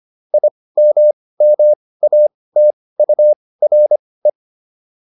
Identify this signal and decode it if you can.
Morse code, heard as a keyed tone, about 25 wpm, spaced as for 18 wpm, 600 Hz. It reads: IMMATURE